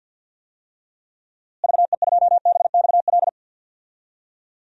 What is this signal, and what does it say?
Morse code, heard as a keyed tone, about 50 wpm, 700 Hz. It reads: VE2BXL
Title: DXCC-100 Call Signs - 6 Characters - Repeated 2x - 50wpm